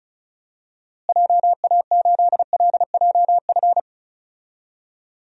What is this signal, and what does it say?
Morse code, heard as a keyed tone, about 36 wpm, 700 Hz. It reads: JA8LJF
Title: DXCC-100 Call Signs - 6 Characters - Repeated 3x - 35wpm